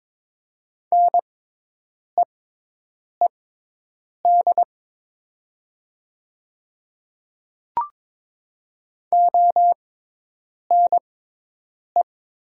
Morse code, heard as a keyed tone, about 22 wpm, spaced as for 8 wpm, 700 Hz. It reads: NEED ONE
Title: 6x Character Spacing - Top 200 Words - 22wpm